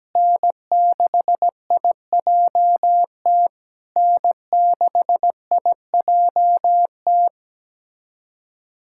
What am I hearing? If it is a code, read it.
Morse code, heard as a keyed tone, about 17 wpm, 700 Hz. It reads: N6IJT N6IJT
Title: SOTA QSO Practice 17wpm